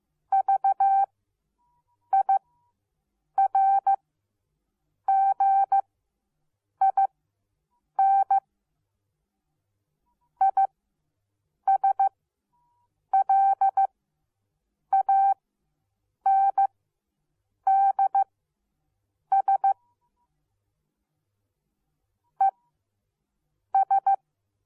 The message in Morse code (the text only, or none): VIRGIN ISLANDS ES